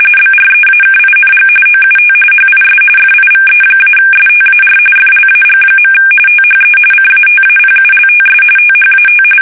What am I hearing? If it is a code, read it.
RTTY, about 45.45 baud. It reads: BREAKING OUT HIS FIVE-STAR FLAG. THE CATAPULTS THAT
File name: fsk850 / battleshipnj